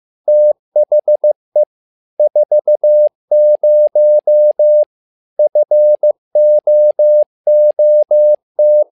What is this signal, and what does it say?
Morse code, heard as a keyed tone, about 15 wpm, 600 Hz. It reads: THE 40 FOOT
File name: learning center / tower trouble / tower trouble 15wpm